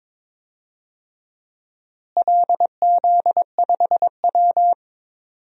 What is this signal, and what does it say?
Morse code, heard as a keyed tone, about 22 wpm, 700 Hz. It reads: LZ5W
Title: DXCC-100 Call Signs - 4 Characters - Repeated 2x - 22wpm